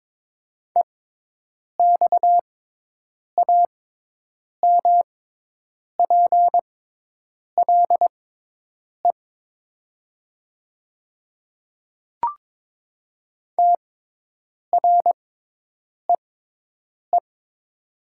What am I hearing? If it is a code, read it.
Morse code, heard as a keyed tone, about 22 wpm, spaced as for 8 wpm, 700 Hz. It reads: EXAMPLE TREE
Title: 6x Character Spacing - Top 300 Words - 22wpm